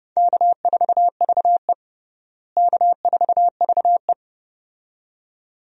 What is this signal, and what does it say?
Morse code, heard as a keyed tone, about 32 wpm, 700 Hz. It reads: K4VE K4VE
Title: USA Call Signs - 4 Characters - Repeated 2x - 30wpm